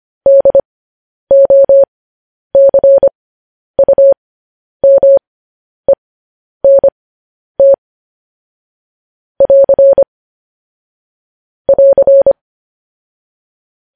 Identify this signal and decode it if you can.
Morse code, heard as a keyed tone, about 25 wpm, spaced as for 10 wpm, 550 Hz. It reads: DOCUMENT <AR> <AR>
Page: news headlines in morse code at 10 wpm